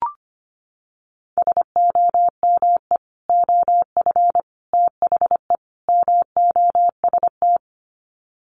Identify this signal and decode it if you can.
Morse code, heard as a keyed tone, about 25 wpm, 700 Hz. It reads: SOME OF THE MOST
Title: Sets of 4 Words 25wpm